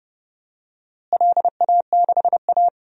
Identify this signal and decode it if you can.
Morse code, heard as a keyed tone, about 30 wpm, 700 Hz. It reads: LA6A
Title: DXCC-100 Call Signs - 4 Characters - Repeated 3x - 30wpm